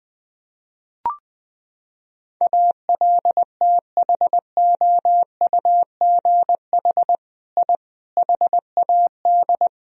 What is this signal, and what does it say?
Morse code, heard as a keyed tone, about 20 wpm, 700 Hz. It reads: ALTHOUGH I HAD
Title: Sentences from Top 1000 Words - Review Part 3 - 20wpm